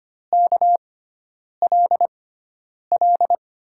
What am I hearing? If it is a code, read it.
Morse code, heard as a keyed tone, about 25 wpm, spaced as for 9 wpm, 700 Hz. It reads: KLL
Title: USA Call Signs - 6 Characters - 6x Character Spacing - 25wpm